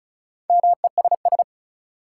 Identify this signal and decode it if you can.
Morse code, heard as a keyed tone, about 36 wpm, 700 Hz. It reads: MESS
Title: Four Letter Words - 35wpm